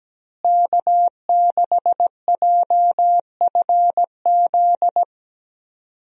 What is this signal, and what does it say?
Morse code, heard as a keyed tone, about 17 wpm, 700 Hz. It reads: K6JFZ
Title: USA Call Signs - 5 Characters - Repeated 2x - 17wpm